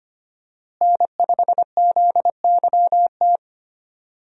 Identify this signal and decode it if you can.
Morse code, heard as a keyed tone, about 25 wpm, 700 Hz. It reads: N5ZYT
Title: USA Call Signs - 5 Characters - Repeated 2x - 25wpm